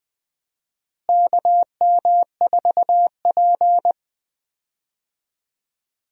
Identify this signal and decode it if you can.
Morse code, heard as a keyed tone, about 21 wpm, 700 Hz. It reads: KM4P
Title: USA Call Signs - 4 Characters - Repeated 3x - 20wpm